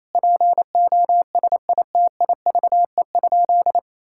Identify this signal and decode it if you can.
Morse code, heard as a keyed tone, about 28 wpm, 700 Hz. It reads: POSITIVE?
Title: Sentences from Top 1000 Words - Review Part 5 - 28wpm